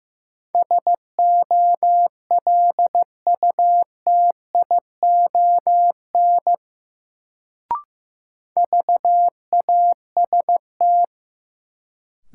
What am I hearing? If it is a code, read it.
Morse code, heard as a keyed tone, about 15 wpm, 700 Hz. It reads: SOLUTION VAST
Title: Sentences from Top 1300 Words - New Word Drill - 15wpm